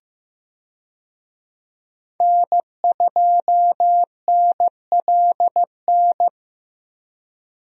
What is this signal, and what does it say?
Morse code, heard as a keyed tone, about 15 wpm, 700 Hz. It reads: N2NLN
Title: USA Call Signs - 5 Characters - Repeated 2x - 15wpm